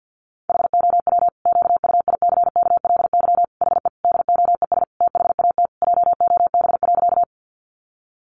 Morse code, heard as a keyed tone, about 50 wpm, 700 Hz. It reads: HOW QUICKLY HE DOES THAT JOB.
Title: Sentences from Top 700 Words 50wpm